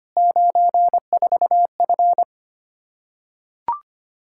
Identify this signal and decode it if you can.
Morse code, heard as a keyed tone, about 25 wpm, 700 Hz. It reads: 94F
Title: K1USN SST QSO Practice 25wpm